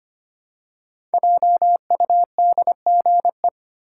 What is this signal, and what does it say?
Morse code, heard as a keyed tone, about 25 wpm, 700 Hz. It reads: JUDGE
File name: 5 Letter Words - First 3 Letters Given - 25wpm